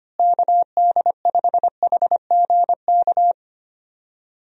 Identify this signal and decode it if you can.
Morse code, heard as a keyed tone, about 25 wpm, 700 Hz. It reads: KD5HGK
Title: USA Call Signs - 6 Characters - Repeated 3x - 25wpm